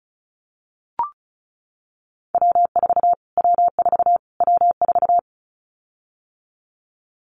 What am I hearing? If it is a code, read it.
Morse code, heard as a keyed tone, about 34 wpm, 700 Hz. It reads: W4 W4 W4
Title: Callsign Prefixes for Top 25 DXCC - Repeated 3x - 35wpm